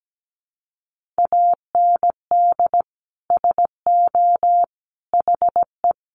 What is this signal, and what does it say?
Morse code, heard as a keyed tone, about 17 wpm, 700 Hz. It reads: AND SO HE
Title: Sentences from Top 1500 Words 17wpm